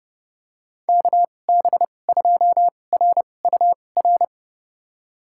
Morse code, heard as a keyed tone, about 29 wpm, spaced as for 22 wpm, 700 Hz. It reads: KB2RUR
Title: USA Call Signs - 6 Characters - 2x Character Spacing - 30wpm